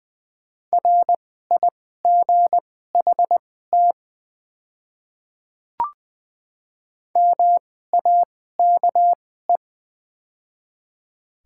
Morse code, heard as a keyed tone, about 20 wpm, spaced as for 14 wpm, 700 Hz. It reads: RIGHT MAKE
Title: 2x Character Spacing - Top 200 Words - 20wpm